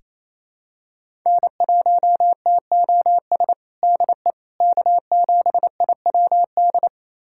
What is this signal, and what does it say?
Morse code, heard as a keyed tone, about 28 wpm, 700 Hz. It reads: N1TOS DE K7IWD
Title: USA Callsign DE Callsign - 5 Characters - 28wpm